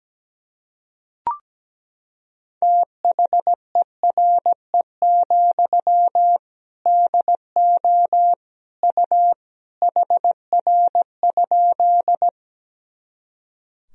Morse code, heard as a keyed tone, about 17 wpm, 700 Hz. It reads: THERE, DO U HR?
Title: Sentences from Top 300 Words and Abbreviations 17wpm